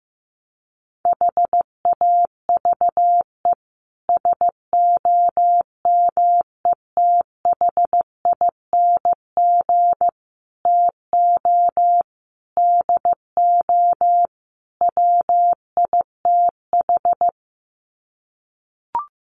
Morse code, heard as a keyed tone, about 15 wpm, 700 Hz. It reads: HAVE SOMETHING TO DO WITH
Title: Sets of 5 Words Encore 15wpm